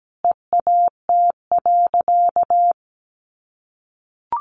Morse code, heard as a keyed tone, about 17 wpm, 700 Hz. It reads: EAT.
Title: Sentences from Top 600 Words 17wpm